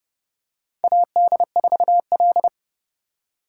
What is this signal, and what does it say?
Morse code, heard as a keyed tone, about 30 wpm, 690 Hz. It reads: AD4L
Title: USA Call Signs - 4 Characters - 2x Character Spacing - 30wpm